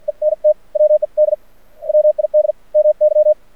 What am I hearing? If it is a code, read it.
Morse code, heard as a keyed tone, about 54 wpm, spaced as for 45 wpm, 610 Hz. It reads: ENT WED WID MY
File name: archived / CRSnet